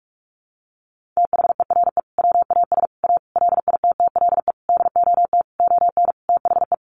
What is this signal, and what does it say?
Morse code, heard as a keyed tone, about 44 wpm, 700 Hz. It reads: THERE WAS A LITTLE DOT ON THE
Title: Sentences from Top 1500 Words 45wpm